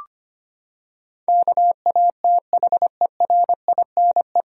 Morse code, heard as a keyed tone, about 25 wpm, 700 Hz. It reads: KATHERINE
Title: USA Common Names - 25wpm